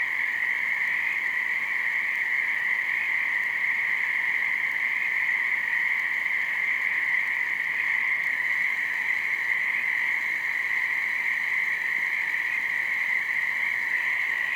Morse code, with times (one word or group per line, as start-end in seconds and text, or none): none